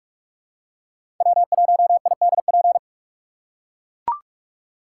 Morse code, heard as a keyed tone, about 42 wpm, 700 Hz. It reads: W1IDP
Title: USA Call Signs - 2x Character Spacing - 45wpm